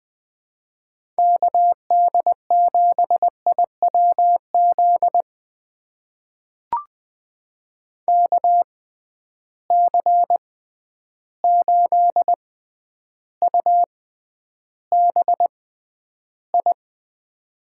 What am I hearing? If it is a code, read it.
Morse code, heard as a keyed tone, about 20 wpm, spaced as for 10 wpm, 700 Hz. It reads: KD7IWZ KC8UBI
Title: USA Call Signs - 6 Characters - 6x Character Spacing - 20wpm